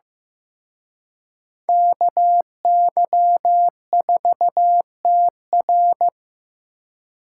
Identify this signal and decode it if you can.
Morse code, heard as a keyed tone, about 18 wpm, 700 Hz. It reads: KY4TR
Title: Speed Racing - USA 2x2 Call Signs 15wpm